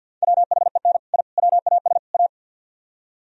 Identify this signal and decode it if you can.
Morse code, heard as a keyed tone, about 59 wpm, 690 Hz. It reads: WHEN I WAS A
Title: Speed-Racing - Sets of 4 Words 50wpm